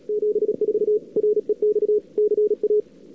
Morse code, heard as a keyed tone, about 35 wpm, 410 Hz. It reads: 74 REX CA
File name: cw-sprint-k5tr-op-n5rz